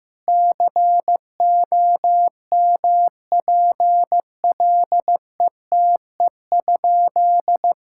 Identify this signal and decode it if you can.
Morse code, heard as a keyed tone, about 15 wpm, 700 Hz. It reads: COMPLETE?
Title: Sentences from Top 1000 Words - Review Part 6 - 15wpm